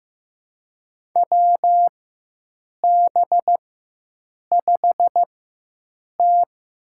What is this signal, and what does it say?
Morse code, heard as a keyed tone, about 15 wpm, spaced as for 7 wpm, 700 Hz. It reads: WB5T
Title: USA Call Signs - 6 Characters - 4x Character Spacing - 15wpm